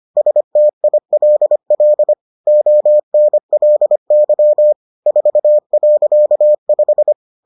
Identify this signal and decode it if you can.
Morse code, heard as a keyed tone, about 25 wpm, 600 Hz. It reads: STILL ONLY 4.5